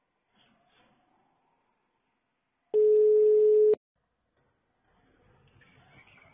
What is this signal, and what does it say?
Ringback tone (Europe, 425 Hz).